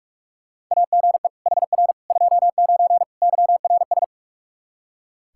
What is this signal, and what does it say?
Morse code, heard as a keyed tone, about 46 wpm, 700 Hz. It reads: AGE HR 29 YRS
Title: Practice QSO - Exchange 3 - More Info - 45wpm